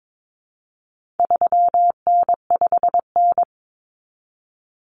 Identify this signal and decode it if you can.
Morse code, heard as a keyed tone, about 22 wpm, 700 Hz. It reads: 3N5N
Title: Common Cut Numbers - 4-Digit 22wpm